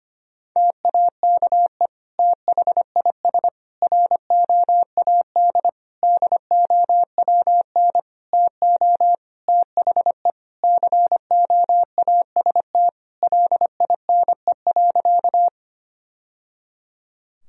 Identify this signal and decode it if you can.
Morse code, heard as a keyed tone, about 25 wpm, 700 Hz. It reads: TAKE THIS ROAD DOWN TO THE COAST LINE.
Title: Sentences from Top 1000 Words - Review Part 4 - 25wpm